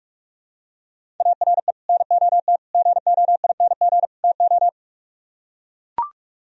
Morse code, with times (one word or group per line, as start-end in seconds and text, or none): 1.2-1.7 ARE
1.9-2.6 NOT
2.7-4.0 GOING
4.2-4.7 TO